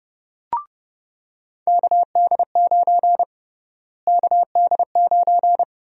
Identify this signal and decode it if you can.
Morse code, heard as a keyed tone, about 32 wpm, 700 Hz. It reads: KD9 KD9
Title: USA Call Sign Prefixes - Repeated 2x - 30wpm